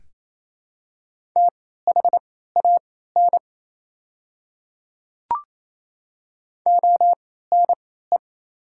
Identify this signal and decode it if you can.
Morse code, heard as a keyed tone, about 28 wpm, spaced as for 16 wpm, 700 Hz. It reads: THAN ONE
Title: 3x Character Spacing - Top 100 Words - 28wpm